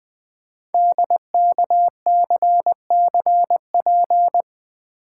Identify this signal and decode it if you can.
Morse code, heard as a keyed tone, about 20 wpm, 700 Hz. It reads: DKCCP
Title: Code Groups - 5 Letters - 20wpm